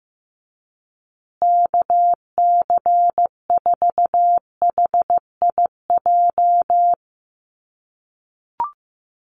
Morse code, heard as a keyed tone, about 15 wpm, 700 Hz. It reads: KC4HIJ
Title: USA Call Signs Repeated 2x 15wpm